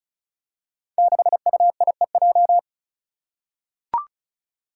Morse code, heard as a keyed tone, about 36 wpm, 700 Hz. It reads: 6UIEJ
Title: Five Challenging Letter-Number Combinations 35wpm